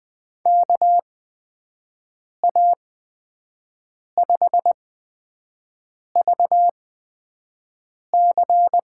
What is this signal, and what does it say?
Morse code, heard as a keyed tone, about 20 wpm, spaced as for 5 wpm, 700 Hz. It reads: KA5VC
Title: USA Call Signs - 6 Characters - 8x Character Spacing - 20wpm